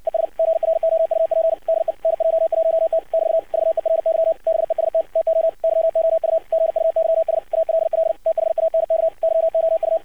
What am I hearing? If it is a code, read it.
Morse code, heard as a keyed tone, about 54 wpm, 650 Hz. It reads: ES GROUP DE NJ1T <BT> VERY BEST TO YOU GUYS ALL THANK YOU
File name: archived / CRSnet